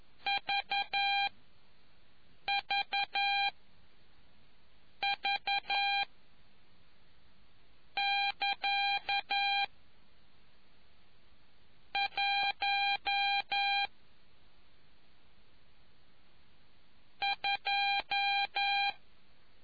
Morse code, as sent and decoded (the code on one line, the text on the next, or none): ...- ...- ...- / -.-.- / .---- / ..---
VVV <KA> 1 2